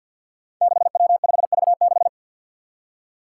Morse code, heard as a keyed tone, about 50 wpm, 700 Hz. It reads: 6W5V6
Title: Five Challenging Letter-Number Combinations 50wpm